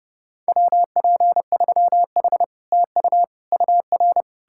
Morse code, heard as a keyed tone, about 40 wpm, 700 Hz. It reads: WP3H TU UR